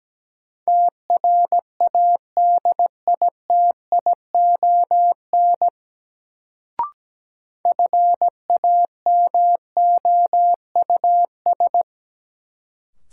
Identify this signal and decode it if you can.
Morse code, heard as a keyed tone, about 17 wpm, 700 Hz. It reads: TRADITION FAMOUS
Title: Sentences from Top 1500 Words - New Word Drill - 17wpm